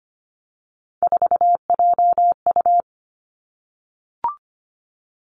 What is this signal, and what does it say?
Morse code, heard as a keyed tone, about 25 wpm, 700 Hz. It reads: 4JU
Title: Three Challenging Letter-Number Combinations 25wpm